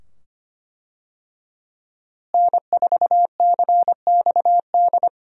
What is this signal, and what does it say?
Morse code, heard as a keyed tone, about 25 wpm, 700 Hz. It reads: N4CXD
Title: USA Call Signs - 5 Characters - Repeated 2x - 25wpm